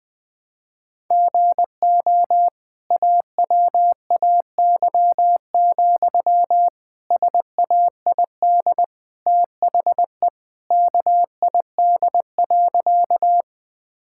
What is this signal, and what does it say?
Morse code, heard as a keyed tone, about 21 wpm, 700 Hz. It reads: GO AWAY, SAID THE KID.
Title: Sentences from Top 1600 Words 20wpm